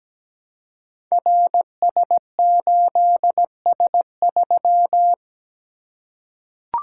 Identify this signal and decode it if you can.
Morse code, heard as a keyed tone, about 17 wpm, 700 Hz. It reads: RS8S3
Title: Code Groups - 5 Letter:Numbers - 17wpm